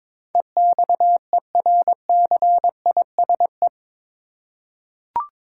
Morse code, heard as a keyed tone, about 22 wpm, 700 Hz. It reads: EXERCISE
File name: practice / 22wpm / Eight Letter Words - 22wpm